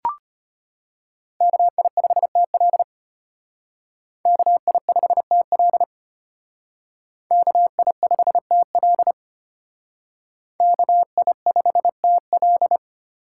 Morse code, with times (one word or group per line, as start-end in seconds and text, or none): 1.4-2.8 KI5TL
4.3-5.8 KI5TL
7.3-9.1 KI5TL
10.6-12.8 KI5TL